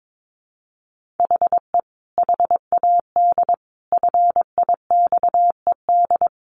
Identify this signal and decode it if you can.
Morse code, heard as a keyed tone, about 22 wpm, 700 Hz. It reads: HE HAD FIXED
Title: Sentences from Top 1500 Words 22wpm